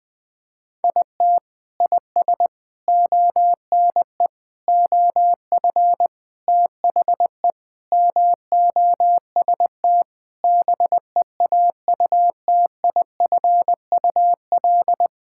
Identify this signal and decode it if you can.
Morse code, heard as a keyed tone, about 21 wpm, 700 Hz. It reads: IT IS ONE OF THE MOST BEAUTIFUL
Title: Sentences with 3-Word Phrases - 20wpm